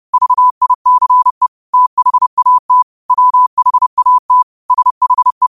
Morse code, heard as a keyed tone, about 30 wpm, 1000 Hz. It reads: VICE THAT WHAT SHE
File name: qotdfiles / 30wpm